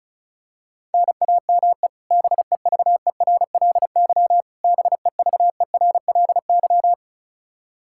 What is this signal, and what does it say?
Morse code, heard as a keyed tone, about 34 wpm, 700 Hz. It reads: NAME BEVERLY BEVERLY
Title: Practice QSO - Exchange 1 - Beginning - 35wpm